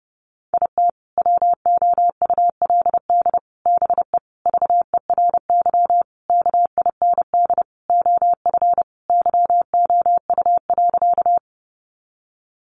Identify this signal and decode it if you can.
Morse code, heard as a keyed tone, about 30 wpm, 700 Hz. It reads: IT WOULD BE VERY KIND OF YOU.
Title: Sentences from Top 200 Words 30wpm